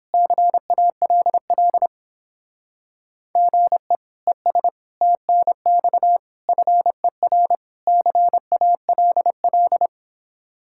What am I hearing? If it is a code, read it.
Morse code, heard as a keyed tone, about 31 wpm, 700 Hz. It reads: CALL GE ES TNX FER CALL